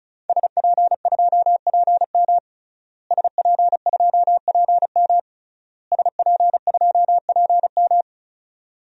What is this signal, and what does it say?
Morse code, heard as a keyed tone, about 36 wpm, 700 Hz. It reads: SP2PM SP2PM SP2PM
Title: DXCC-100 Call Signs - 5 Characters - Repeated 3x - 35wpm